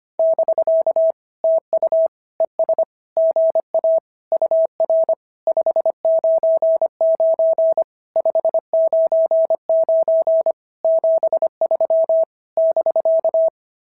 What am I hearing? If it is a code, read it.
Morse code, heard as a keyed tone, about 24 wpm, 650 Hz. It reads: <BK> TU ES GA UR 599 599 73 <BK>
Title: SOTA QSO Practice 25wpm